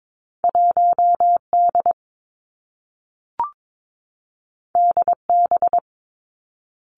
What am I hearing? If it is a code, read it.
Morse code, heard as a keyed tone, about 22 wpm, 700 Hz. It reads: 1 D D B
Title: Two Challenging Letter-Number Combinations 22wpm